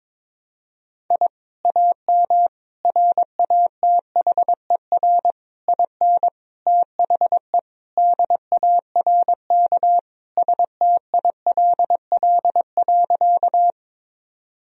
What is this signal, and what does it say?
Morse code, heard as a keyed tone, about 22 wpm, 700 Hz. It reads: I AM RATHER IN THE DARK STILL.
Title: Sentences from Top 1000 Words - Review Part 4 - 22wpm